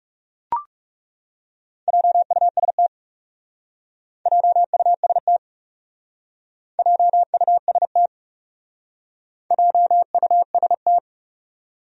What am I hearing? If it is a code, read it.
Morse code, heard as a keyed tone, about 35 wpm, 700 Hz. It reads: JUST JUST JUST JUST
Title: Speed-Racing Top 200 Words 30wpm